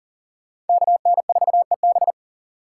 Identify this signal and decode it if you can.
Morse code, heard as a keyed tone, about 52 wpm, 700 Hz. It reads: KN4EB